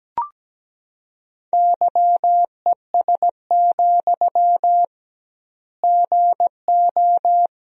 Morse code, heard as a keyed tone, about 17 wpm, 700 Hz. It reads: YES, GO
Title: 2x Word-Spacing - Sentences from Top 500 Words - 17wpm